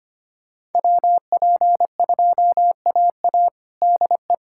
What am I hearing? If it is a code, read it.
Morse code, heard as a keyed tone, about 25 wpm, 700 Hz. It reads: WP2AA DE
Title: USA Callsign DE Callsign - 5 Characters - 25wpm